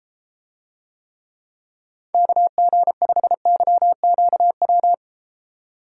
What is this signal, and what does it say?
Morse code, heard as a keyed tone, about 34 wpm, 700 Hz. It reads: KG5YQW